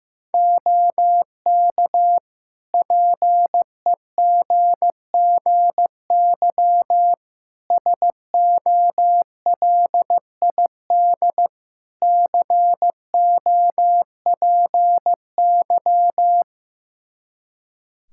Morse code, heard as a keyed tone, about 15 wpm, 700 Hz. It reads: OK PEGGY SOLID COPY
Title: Practice QSO - Exchange 3 - More Info - 15wpm